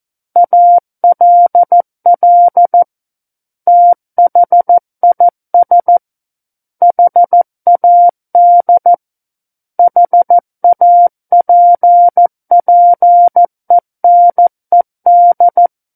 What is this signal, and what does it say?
Morse code, heard as a keyed tone, about 14 wpm, 700 Hz. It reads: ALL THIS HAD HAPPENED
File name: Book One / 05-Chapter 5 - 14 WPM